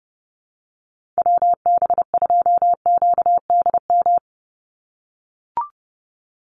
Morse code, heard as a keyed tone, about 30 wpm, 700 Hz. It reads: WB2QDM
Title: USA Call Signs 30wpm